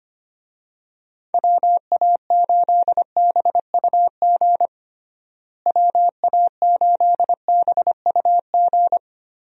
Morse code, heard as a keyed tone, about 25 wpm, 700 Hz. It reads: WA8BUG WA8BUG
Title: USA Call Signs - 6 Characters - Repeated 2x - 25wpm